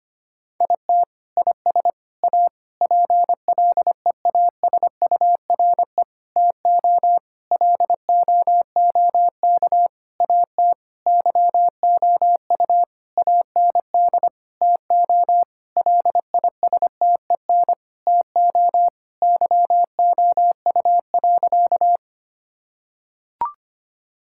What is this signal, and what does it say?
Morse code, heard as a keyed tone, about 25 wpm, 700 Hz. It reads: IT IS A PLEASURE TO LOOK AT YOU AND TO LISTEN TO YOU.
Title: Sentences from Top 1600 Words 25wpm